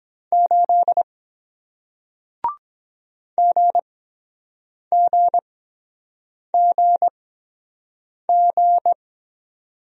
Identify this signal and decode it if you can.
Morse code, heard as a keyed tone, about 22 wpm, spaced as for 7 wpm, 700 Hz. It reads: 8 GGGG